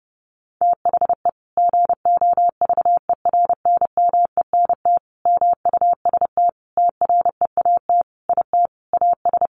Morse code, heard as a keyed tone, about 30 wpm, 700 Hz. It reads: THE GOVERNMENT MUST TREAT IT AS
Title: Sentences from Top 1300 Words 30wpm